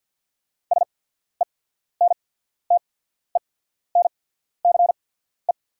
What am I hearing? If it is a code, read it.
Morse code, heard as a keyed tone, about 48 wpm, spaced as for 14 wpm, 690 Hz. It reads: SENTENCE
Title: 8x Character Spacing - Top 200 Words - 50wpm